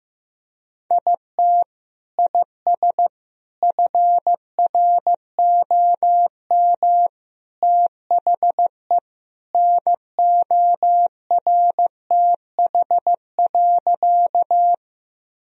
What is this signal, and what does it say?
Morse code, heard as a keyed tone, about 16 wpm, 700 Hz. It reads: IT IS FROM THE NORTH.
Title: Sentences from Top 1000 Words - Review Part 1 - 15wpm